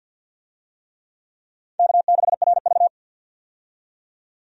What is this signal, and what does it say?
Morse code, heard as a keyed tone, about 53 wpm, 700 Hz. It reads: K6RV
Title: USA Call Signs - 4 Characters - 50wpm